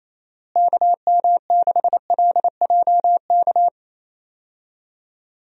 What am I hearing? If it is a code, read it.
Morse code, heard as a keyed tone, about 28 wpm, 700 Hz. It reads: KM6LJK
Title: USA Call Signs - 6 Characters - Repeated 2x - 28wpm